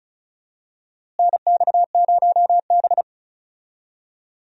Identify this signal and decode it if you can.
Morse code, heard as a keyed tone, about 36 wpm, 700 Hz. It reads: NX0B
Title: USA Call Signs - 4 Characters - Repeated 3x - 35wpm